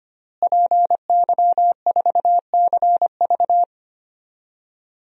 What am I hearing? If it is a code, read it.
Morse code, heard as a keyed tone, about 25 wpm, 700 Hz. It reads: PY4CV
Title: DXCC-100 Call Signs - 5 Characters - Repeated 2x - 25wpm